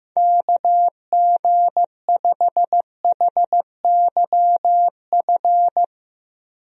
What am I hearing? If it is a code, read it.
Morse code, heard as a keyed tone, about 15 wpm, 700 Hz. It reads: KG5HYF
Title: USA Call Signs - 6 Characters - Repeated 3x - 15wpm